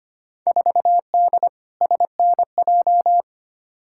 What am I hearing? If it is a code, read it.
Morse code, heard as a keyed tone, about 25 wpm, 700 Hz. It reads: 4D SNJ
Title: Field Day Exchanges Encore - 25wpm